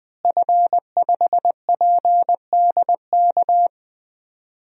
Morse code, heard as a keyed tone, about 20 wpm, 700 Hz. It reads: F5PDK
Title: DXCC-100 Call Signs - 5 Characters - Repeated 3x - 20wpm